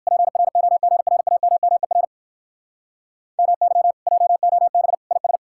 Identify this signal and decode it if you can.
Morse code, heard as a keyed tone, about 57 wpm, 700 Hz. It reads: PROGRAMMER MY JOB IS